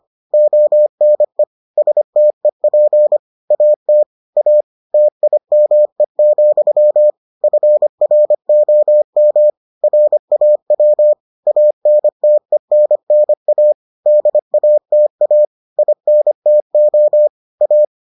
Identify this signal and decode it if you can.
Morse code, heard as a keyed tone, about 25 wpm, 600 Hz. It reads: ONE STEP AT A TIME, FROM RAW ANTENNA DATA INTO A